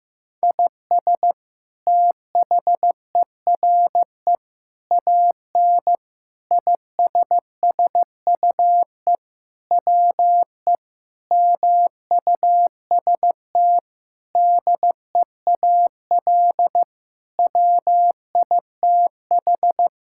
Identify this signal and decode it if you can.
Morse code, heard as a keyed tone, about 15 wpm, 700 Hz. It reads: IS THERE AN ISSUE WE MUST DEAL WITH
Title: Sentences from Top 1000 Words - Review Part 6 - 15wpm